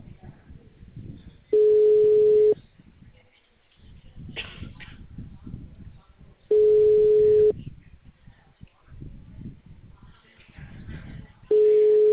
Ringback tone (Europe, 425 Hz).